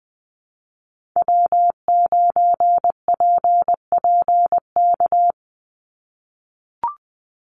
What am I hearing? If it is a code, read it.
Morse code, heard as a keyed tone, about 20 wpm, 700 Hz. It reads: W9PPK
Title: USA Call Signs 20wpm